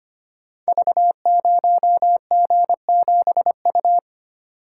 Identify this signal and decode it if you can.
Morse code, heard as a keyed tone, about 25 wpm, 690 Hz. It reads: V0G7U
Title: Code Groups - 5 Letter:Numbers - 25wpm